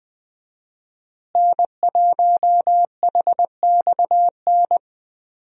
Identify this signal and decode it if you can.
Morse code, heard as a keyed tone, about 20 wpm, 700 Hz. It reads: N1HXN
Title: USA Call Signs - 5 Characters - Repeated 3x - 20wpm